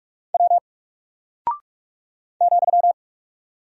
Morse code, heard as a keyed tone, about 44 wpm, 700 Hz. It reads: W ,
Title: Single Character (Letter:Number:Punctuation:Prosign) Alt 45wpm